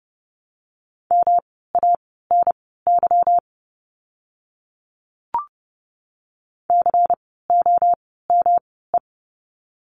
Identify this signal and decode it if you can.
Morse code, heard as a keyed tone, about 30 wpm, spaced as for 17 wpm, 700 Hz. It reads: MANY COME
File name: course / 30:10wpm / 230 Lesson 37f - Letter x - Words 30wpm